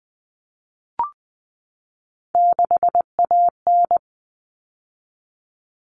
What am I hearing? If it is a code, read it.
Morse code, heard as a keyed tone, about 20 wpm, 700 Hz. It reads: 6AN
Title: All Cut Numbers - 3-Digit 20wpm